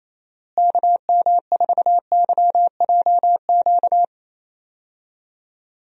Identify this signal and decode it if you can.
Morse code, heard as a keyed tone, about 28 wpm, 700 Hz. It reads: KM4YJQ
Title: USA Call Signs - 6 Characters - Repeated 3x - 28wpm